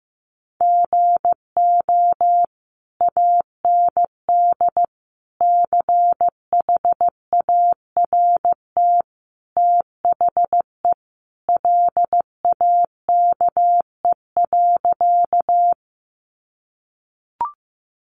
Morse code, heard as a keyed tone, about 15 wpm, 700 Hz. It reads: GO AND CHART THE LAKE.
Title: Sentences from Top 1300 Words 15wpm